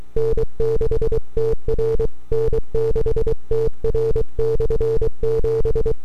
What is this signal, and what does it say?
Morse code, heard as a keyed tone, about 23 wpm, 450 Hz. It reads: N6TR N6TR/7
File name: n6tr / 12Oct2009-160